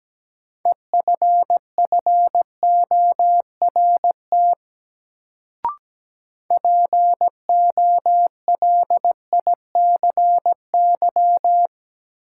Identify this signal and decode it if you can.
Morse code, heard as a keyed tone, about 17 wpm, 700 Hz. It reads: EFFORT POLICY
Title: Sentences from Top 1200 Words - New Word Drill Alt - 17wpm